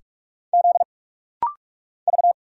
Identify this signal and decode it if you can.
Morse code, heard as a keyed tone, about 44 wpm, 700 Hz. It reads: Z V
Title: Single Letters - Rapid-Fire 45wpm